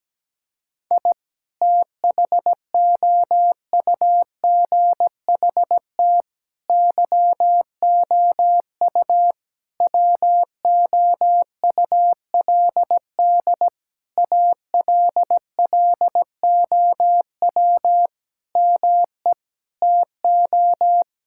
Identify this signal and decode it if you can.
Morse code, heard as a keyed tone, about 17 wpm, 700 Hz. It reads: I THOUGHT YOU WOULD ALLOW ME TO
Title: Sentences from Top 1000 Words - Review Part 5 - 17wpm